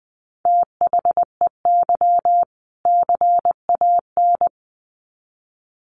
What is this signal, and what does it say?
Morse code, heard as a keyed tone, about 20 wpm, 700 Hz. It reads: THEY CAN
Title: Sets of 2 Words 20wpm